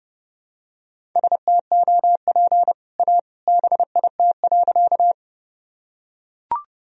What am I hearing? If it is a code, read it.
Morse code, heard as a keyed tone, about 30 wpm, 700 Hz. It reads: STOP A BIT.
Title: Sentences from Top 1000 Words - Review Part 1 - 30wpm